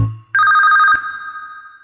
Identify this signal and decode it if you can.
There is no code.